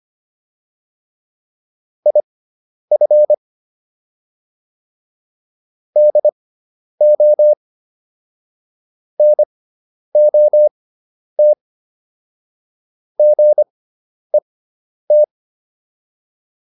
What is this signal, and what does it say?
Morse code, heard as a keyed tone, about 25 wpm, spaced as for 10 wpm, 600 Hz. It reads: IF DO NOT GET
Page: intermediate-practice-files